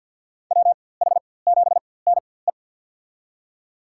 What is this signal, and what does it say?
Morse code, heard as a keyed tone, about 53 wpm, spaced as for 24 wpm, 700 Hz. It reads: WH7NE
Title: USA Call Signs - 5 Characters - 4x Character Spacing - 50wpm